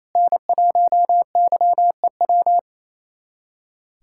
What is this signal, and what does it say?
Morse code, heard as a keyed tone, about 28 wpm, 700 Hz. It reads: N1YEW
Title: USA Callsign DE Callsign 28wpm